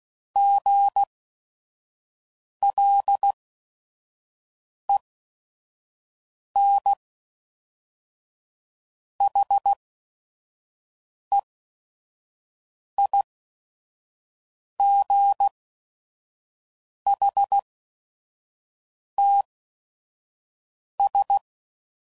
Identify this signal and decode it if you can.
Morse code, heard as a keyed tone, about 16 wpm, spaced as for 5 wpm, 800 Hz. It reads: GLENHEIGHTS
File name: CW Code Groups @5WPM / fromSM's / Cities